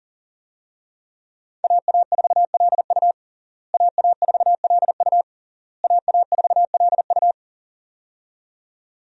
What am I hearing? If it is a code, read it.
Morse code, heard as a keyed tone, about 37 wpm, 700 Hz. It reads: AA4LU AA4LU AA4LU
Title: USA Call Signs - 5 Characters - Repeated 3x - 40wpm